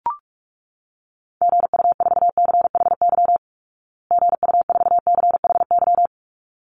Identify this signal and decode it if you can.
Morse code, heard as a keyed tone, about 44 wpm, 700 Hz. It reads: GU4CHY GU4CHY
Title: Call Signs Repeated 3x - 45wpm